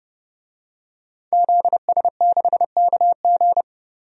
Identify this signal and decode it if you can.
Morse code, heard as a keyed tone, about 31 wpm, 700 Hz. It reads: ZS6KG
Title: DXCC-100 Call Signs - 5 Characters - 30wpm